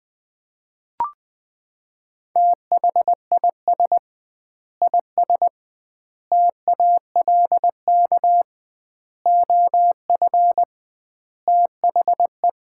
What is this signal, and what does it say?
Morse code, heard as a keyed tone, about 20 wpm, 700 Hz. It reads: THIS IS TALK OF THE
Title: 2x Word-Spacing - Sets of 5 Words - 20wpm